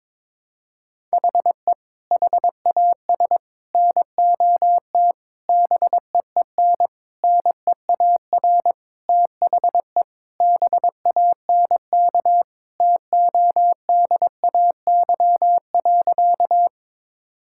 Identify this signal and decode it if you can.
Morse code, heard as a keyed tone, about 22 wpm, 700 Hz. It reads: HE HAS NOT BEEN NEAR THE BANK TODAY.
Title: Sentences from Top 1000 Words - Review Part 6 - 22wpm